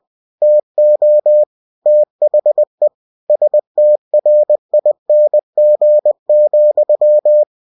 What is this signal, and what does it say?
Morse code, heard as a keyed tone, about 20 wpm, 600 Hz. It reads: TO THE STRING,
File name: cw-books / winnie-the-pooh / winnie-20wpm-page-0006